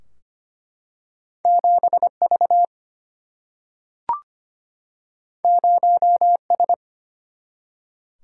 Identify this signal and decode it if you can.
Morse code, heard as a keyed tone, about 25 wpm, 700 Hz. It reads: 7 V 0 S
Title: Single-Letter Single-Number 25wpm